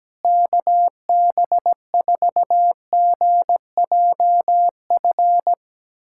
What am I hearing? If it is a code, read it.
Morse code, heard as a keyed tone, about 17 wpm, 700 Hz. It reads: KB4GJF
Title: USA Callsign DE Callsign 17wpm